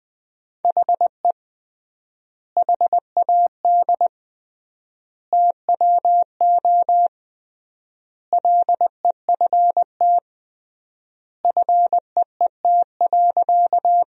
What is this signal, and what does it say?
Morse code, heard as a keyed tone, about 20 wpm, 700 Hz. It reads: HE HAD TWO LEFT FEET.
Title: 3x Word-Spacing - Sentences from Top 300 Words - 20wpm